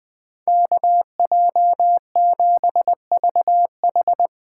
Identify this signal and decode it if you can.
Morse code, heard as a keyed tone, about 22 wpm, 700 Hz. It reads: KJ7VH
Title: Speed Racing - USA 2x2 Call Signs 20wpm